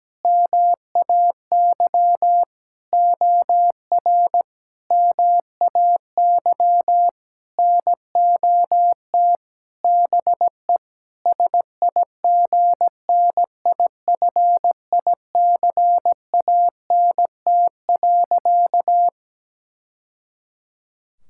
Morse code, heard as a keyed tone, about 17 wpm, 700 Hz. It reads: MAY OR MAY NOT BE SIGNIFICANT.
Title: Sentences from Top 1000 Words - Review Part 6 - 17wpm